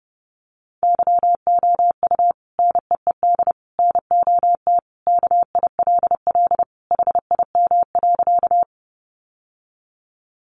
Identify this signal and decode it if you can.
Morse code, heard as a keyed tone, about 30 wpm, 700 Hz. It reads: YOU NEED NOT KILL HIM.
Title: Sentences from Top 1400 Words 30wpm